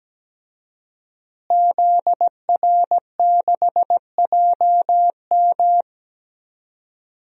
Morse code, heard as a keyed tone, about 17 wpm, 700 Hz. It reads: ZR6JM
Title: DXCC-100 Call Signs - 5 Characters - Repeated 2x - 17wpm